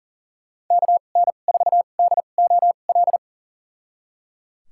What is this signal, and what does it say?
Morse code, heard as a keyed tone, about 37 wpm, spaced as for 28 wpm, 700 Hz. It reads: KN4DOL
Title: USA Call Signs - 6 Characters - 2x Character Spacing - 40wpm